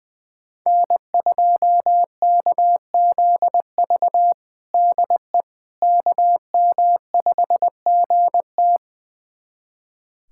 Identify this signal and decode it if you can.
Morse code, heard as a keyed tone, about 20 wpm, 700 Hz. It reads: N2KZV DE KM5GT
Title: USA Callsign DE Callsign - 5 Characters - 20wpm